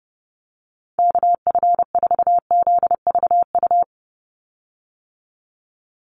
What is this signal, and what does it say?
Morse code, heard as a keyed tone, about 30 wpm, 700 Hz. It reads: KF4ZVU
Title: USA Call Signs 30wpm